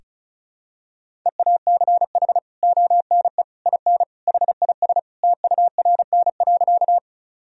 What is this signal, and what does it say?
Morse code, heard as a keyed tone, about 36 wpm, 700 Hz. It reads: EACH ONE IN HIS TURN.
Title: Sentences from Top 1000 Words - Review Part 1 - 35wpm